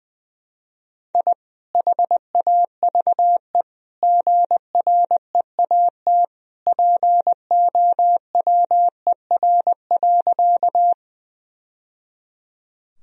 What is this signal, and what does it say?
Morse code, heard as a keyed tone, about 20 wpm, 700 Hz. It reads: I HAVE GREAT POWER.
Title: Sentences from Top 1000 Words - Review Part 2 - 20wpm